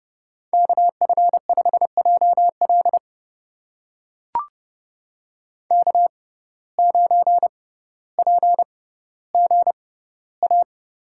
Morse code, heard as a keyed tone, about 31 wpm, spaced as for 16 wpm, 700 Hz. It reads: KF5JL K9PGA
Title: USA Call Signs - 5 Characters - 6x Character Spacing - 30wpm